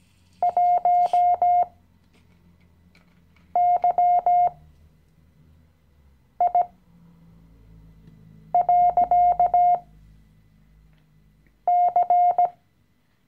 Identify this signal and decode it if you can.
Morse code, heard as a keyed tone, about 17 wpm, spaced as for 4 wpm, 710 Hz. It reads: 1YI.C
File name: CW-Online-Kurs / Anfänger / Lektion 016